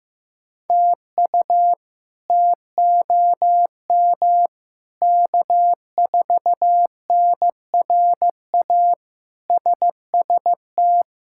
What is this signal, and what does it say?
Morse code, heard as a keyed tone, about 15 wpm, 700 Hz. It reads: TU TOM K4NRA SST
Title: K1USN SST QSO Practice - 15wpm